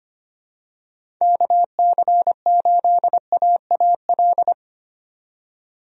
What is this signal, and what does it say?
Morse code, heard as a keyed tone, about 25 wpm, 700 Hz. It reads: KC8AAL
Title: USA Call Signs - 6 Characters - 25wpm